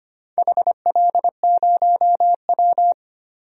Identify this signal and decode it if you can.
Morse code, heard as a keyed tone, about 25 wpm, 700 Hz. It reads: HL0W
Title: DXCC-100 Call Signs - 4 Characters - 25wpm